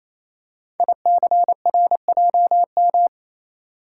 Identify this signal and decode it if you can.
Morse code, heard as a keyed tone, about 27 wpm, 700 Hz. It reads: ICRJM
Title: Code Groups - 5 Letters - 28wpm